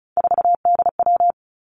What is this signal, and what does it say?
Morse code, heard as a keyed tone, about 36 wpm, 700 Hz. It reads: 4DW
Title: Three Challenging Letter-Number Combinations 35wpm